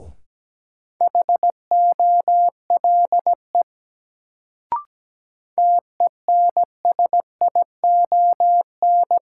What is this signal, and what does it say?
Morse code, heard as a keyed tone, about 17 wpm, 700 Hz. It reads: HOLE TENSION
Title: Sentences from Top 1600 Words - New Word Drill Alt - 17wpm